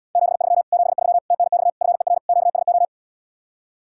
Morse code, heard as a keyed tone, about 78 wpm, 700 Hz. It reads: 73 73 TNX FER QSO